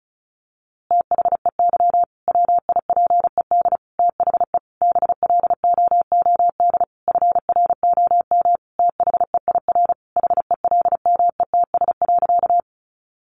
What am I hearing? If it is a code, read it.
Morse code, heard as a keyed tone, about 34 wpm, 700 Hz. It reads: THEY WIPED THE BLOOD FROM THEIR HELMETS.
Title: A Sound Of Thunder 35wpm